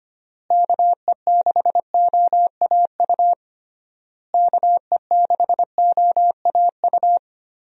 Morse code, heard as a keyed tone, about 25 wpm, 700 Hz. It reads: KE6OAU KE6OAU
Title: USA Call Signs - 6 Characters - Repeated 3x - 25wpm